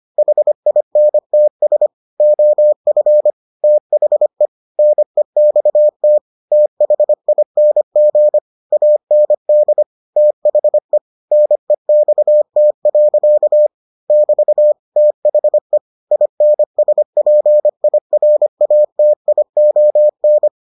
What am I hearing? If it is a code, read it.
Morse code, heard as a keyed tone, about 25 wpm, 600 Hz. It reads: HINTS OF THE NEXT THING AND THE NEXT. <BT> THE INSPIRATION